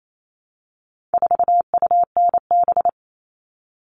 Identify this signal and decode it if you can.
Morse code, heard as a keyed tone, about 28 wpm, 700 Hz. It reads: 4UNB
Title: All Cut Numbers - 4-Digit 28wpm